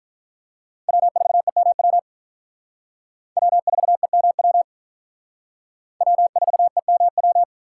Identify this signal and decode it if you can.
Morse code, heard as a keyed tone, about 44 wpm, 700 Hz. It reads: W4EMW W4EMW W4EMW